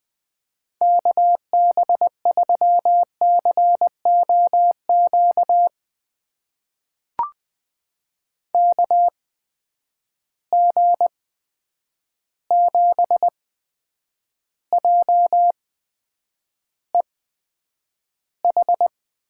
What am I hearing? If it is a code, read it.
Morse code, heard as a keyed tone, about 20 wpm, spaced as for 9 wpm, 700 Hz. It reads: KB3COQ KG7JEH
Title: USA Call Signs - 6 Characters - 8x Character Spacing - 20wpm